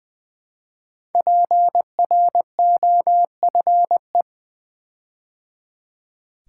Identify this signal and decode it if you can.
Morse code, heard as a keyed tone, about 20 wpm, 700 Hz. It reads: PROFE
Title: Build A Word - 5 Letters At A Time - 12-Letter Words - 20wpm